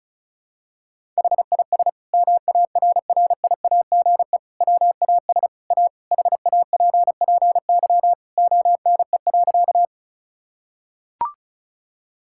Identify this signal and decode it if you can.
Morse code, heard as a keyed tone, about 36 wpm, 700 Hz. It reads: HIS MARRIAGE WAS A HAPPY ONE.
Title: Sentences from Top 1000 Words - Review Part 5 - 35wpm